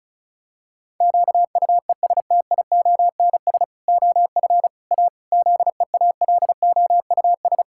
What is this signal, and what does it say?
Morse code, heard as a keyed tone, about 36 wpm, 700 Hz. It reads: QUESTIONS OF A ZEALOUS
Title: Pangram Challenge v2 - 35wpm